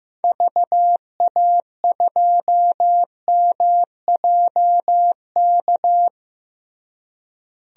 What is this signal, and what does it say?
Morse code, heard as a keyed tone, about 15 wpm, 700 Hz. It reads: VA2MJK
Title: DXCC-100 Call Signs - 6 Characters - Repeated 3x - 15wpm